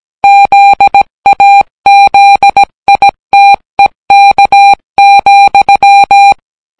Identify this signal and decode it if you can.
Morse code, heard as a keyed tone, about 17 wpm, 800 Hz. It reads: ZAZITEK,